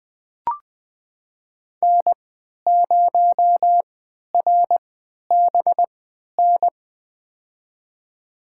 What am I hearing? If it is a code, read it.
Morse code, heard as a keyed tone, about 20 wpm, spaced as for 11 wpm, 700 Hz. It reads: N0RBN
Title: USA Call Signs - 5 Characters - 3x Character Spacing - 20wpm